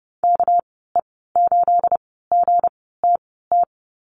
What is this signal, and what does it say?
Morse code, heard as a keyed tone, about 30 wpm, spaced as for 17 wpm, 700 Hz. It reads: KE8GTT
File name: Lesson 37g - Letter x - Call Signs 30wpm